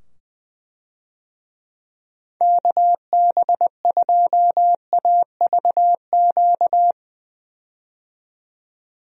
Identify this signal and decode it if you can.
Morse code, heard as a keyed tone, about 20 wpm, 700 Hz. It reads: KB2AVQ
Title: USA Call Signs - 6 Characters - 20wpm